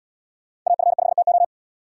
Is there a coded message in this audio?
Yes: Morse code from 0.7 to 1.4 s, heard as a keyed tone, about 77 wpm, 700 Hz. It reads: AF5TO